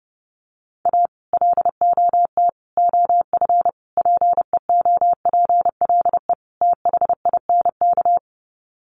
Morse code, heard as a keyed tone, about 30 wpm, 700 Hz. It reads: A LOT OF PEOPLE THINK
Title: Sets of 5 Words Encore 30wpm